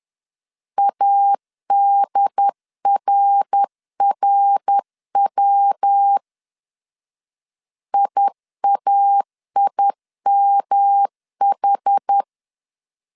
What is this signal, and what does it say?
Morse code, heard as a keyed tone, about 10 wpm, 800 Hz. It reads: ADRRW IAIMH
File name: CW / DET / 1 3 1500C S B 10wpm 3335